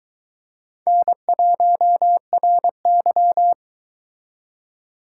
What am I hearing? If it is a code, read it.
Morse code, heard as a keyed tone, about 23 wpm, 700 Hz. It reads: N1RY